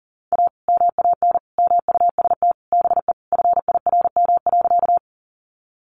Morse code, heard as a keyed tone, about 40 wpm, 700 Hz. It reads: A MAN MUST BE FIRM.
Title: Sentences from Top 1400 Words 40wpm